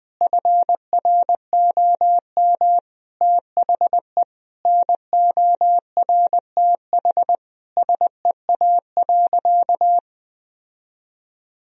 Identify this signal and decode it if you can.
Morse code, heard as a keyed tone, about 20 wpm, 700 Hz. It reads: FROM THE NORTH SEA.
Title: Sentences from Top 1000 Words - Review Part 1 - 20wpm